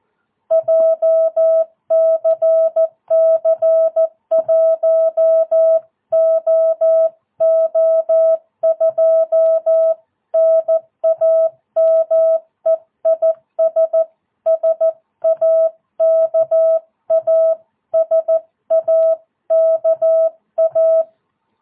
Morse code, heard as a keyed tone, about 14 wpm, 660 Hz. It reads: JCC 1OO2 NAME IS SAKA SAKA
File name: jh8cbh / 2022cw